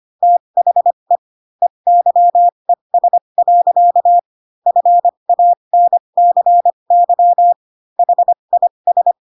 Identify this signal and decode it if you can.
Morse code, heard as a keyed tone, about 25 wpm, spaced as for 22 wpm, 700 Hz. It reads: THE EYES. FANCY HIS
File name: Pinocchio / Pinocchio ch3 18 25 ik1hge